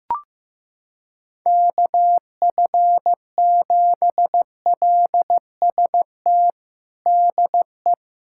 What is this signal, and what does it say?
Morse code, heard as a keyed tone, about 15 wpm, 700 Hz. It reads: KF7LST DE
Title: USA Callsign DE Callsign 15wpm